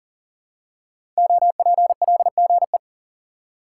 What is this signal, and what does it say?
Morse code, heard as a keyed tone, about 39 wpm, 700 Hz. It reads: OPLGE
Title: Code Groups - 5 Letters - 40wpm